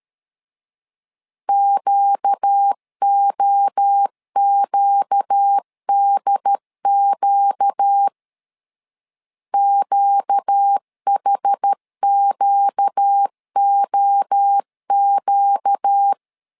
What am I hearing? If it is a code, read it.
Morse code, heard as a keyed tone, about 13 wpm, 800 Hz. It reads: QOQDQ QHQOQ
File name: LEZIONI / 4 Q 12wpm 1833